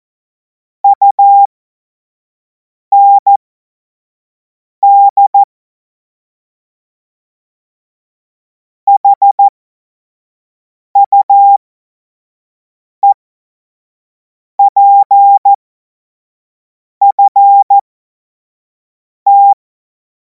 Morse code, heard as a keyed tone, about 14 wpm, spaced as for 5 wpm, 800 Hz. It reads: UND HUEPFT
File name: Morsekurs / Witze / Witz 14. Juli 2026 25 F70